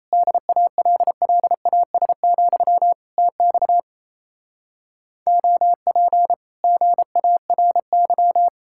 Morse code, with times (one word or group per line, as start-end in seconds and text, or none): 0.1-2.9 DALLAS,
3.2-3.8 TX
5.3-6.3 OP
6.6-8.5 GARY